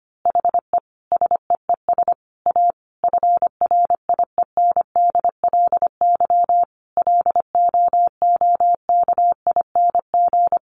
Morse code, heard as a keyed tone, about 25 wpm, 700 Hz. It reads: HE SEES A FRIENDLY LOOKING
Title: Easy Phrases 25wpm